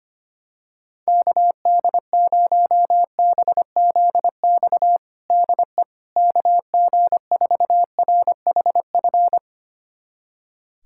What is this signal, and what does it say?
Morse code, heard as a keyed tone, about 25 wpm, 700 Hz. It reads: KD0BZX DE KG4RHF
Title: USA Callsign DE Callsign - 6 Characters - 25wpm